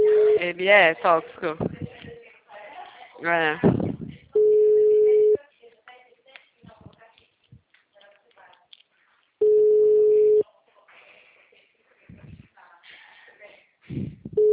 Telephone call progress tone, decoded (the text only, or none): ringback tone